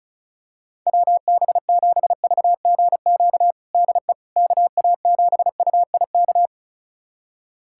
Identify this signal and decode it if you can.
Morse code, heard as a keyed tone, about 34 wpm, 690 Hz. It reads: WB7VGQ DE KA7UIK
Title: USA Callsign DE Callsign - 6 Characters - 35wpm